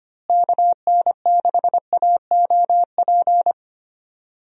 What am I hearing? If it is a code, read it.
Morse code, heard as a keyed tone, about 25 wpm, 700 Hz. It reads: KN6AOP
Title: USA Call Signs - 6 Characters - Repeated 2x - 25wpm